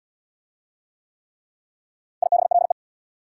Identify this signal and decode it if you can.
Morse code, heard as a keyed tone, about 67 wpm, 690 Hz. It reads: IBLE